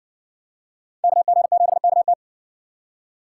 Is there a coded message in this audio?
Yes: Morse code from 1.0 to 2.1 s, heard as a keyed tone, about 52 wpm, 700 Hz. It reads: KG7KT